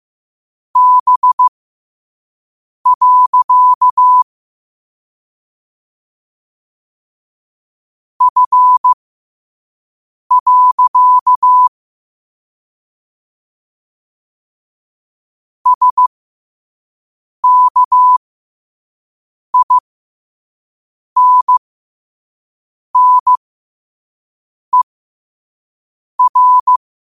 Morse code, heard as a keyed tone, about 15 wpm, spaced as for 5 wpm, 1000 Hz. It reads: B. F. SKINNER
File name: qotdfiles / 5wpm